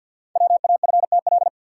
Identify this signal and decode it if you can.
Morse code, heard as a keyed tone, about 45 wpm, 690 Hz. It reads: WAFTL
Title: Code Groups - 5 Letters - 50wpm